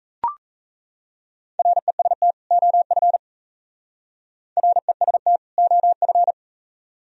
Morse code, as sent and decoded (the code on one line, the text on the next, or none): .-. ... - / --- ..-. / .-. ... - / --- ..-.
REST OF REST OF